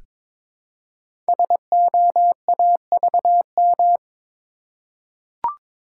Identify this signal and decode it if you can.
Morse code, heard as a keyed tone, about 22 wpm, 700 Hz. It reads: SOAVM
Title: Code Groups - 5 Letters - 22wpm